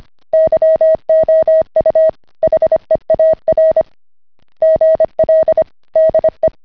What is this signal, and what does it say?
Morse code, heard as a keyed tone, about 25 wpm, 640 Hz. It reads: YOU HEAR GL DE